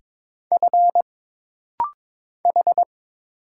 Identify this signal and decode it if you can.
Morse code, heard as a keyed tone, about 22 wpm, 700 Hz. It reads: F H
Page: This rapid-fire timing is geared to push your ICR (Instant Character Recognition) skills.